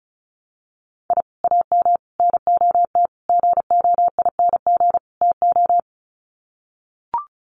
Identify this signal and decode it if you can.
Morse code, heard as a keyed tone, about 34 wpm, 700 Hz. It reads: I AM NOT GOING TO
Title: Sets of 5 Words Encore 35wpm